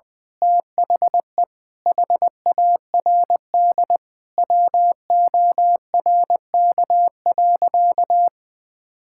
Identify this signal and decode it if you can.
Morse code, heard as a keyed tone, about 20 wpm, 700 Hz. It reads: THE HARD WORK.
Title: Sentences with 4-Word Phrases - 20wpm